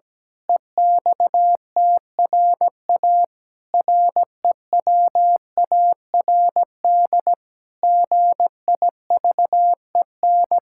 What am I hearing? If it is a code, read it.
Morse code, heard as a keyed tone, about 17 wpm, 700 Hz. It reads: EXTRA REWARD GIVEN
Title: Pangram Challenge v2 - 17wpm